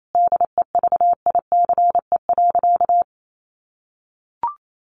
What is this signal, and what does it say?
Morse code, heard as a keyed tone, about 28 wpm, 700 Hz. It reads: DEVICE.
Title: Sentences from Top 1500 Words 28wpm